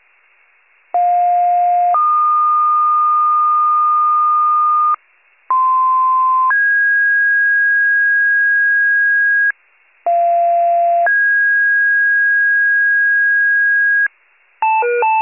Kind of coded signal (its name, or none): two-tone page